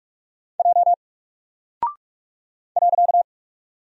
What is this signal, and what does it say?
Morse code, heard as a keyed tone, about 43 wpm, 700 Hz. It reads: J .
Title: Single Character (Letter:Number:Punctuation:Prosign) Alt 45wpm